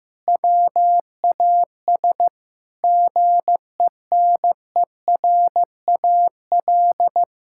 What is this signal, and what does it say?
Morse code, heard as a keyed tone, about 15 wpm, 700 Hz. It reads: WAS GENERAL
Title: Sentences from Top 1600 Words 15wpm